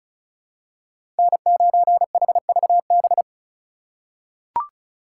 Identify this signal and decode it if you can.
Morse code, heard as a keyed tone, about 36 wpm, 700 Hz. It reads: N9HVB
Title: USA Call Signs - 5 Characters - 8x Character Spacing - 35wpm